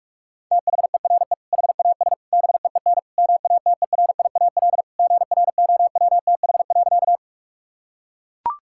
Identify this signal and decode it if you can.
Morse code, heard as a keyed tone, about 41 wpm, 700 Hz. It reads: THERE HAS BEEN MATERIAL GROWTH.
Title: Sentences from Top 1000 Words - Review Part 5 - 45wpm